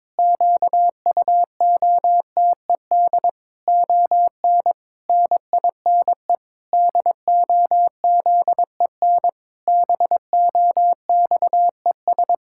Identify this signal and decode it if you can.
Morse code, heard as a keyed tone, about 22 wpm, 700 Hz. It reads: QUOTED ON NINE DOZEN BOXES
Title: Pangram Challenge v2 - 22wpm